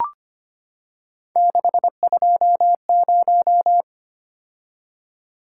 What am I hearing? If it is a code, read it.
Morse code, heard as a keyed tone, about 25 wpm, 700 Hz. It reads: 620
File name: Three Letter-Number Combinations - 25wpm